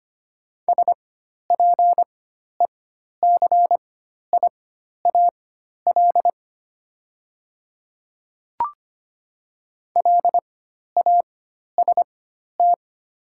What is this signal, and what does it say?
Morse code, heard as a keyed tone, about 25 wpm, spaced as for 12 wpm, 700 Hz. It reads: SPECIAL LAST
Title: 4x Character Spacing - Top 500 Words - 25wpm